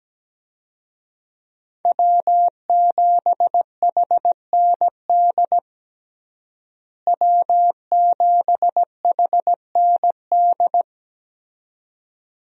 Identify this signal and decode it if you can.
Morse code, heard as a keyed tone, about 17 wpm, 700 Hz. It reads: W7HND W7HND
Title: USA Call Signs - 5 Characters - Repeated 2x - 17wpm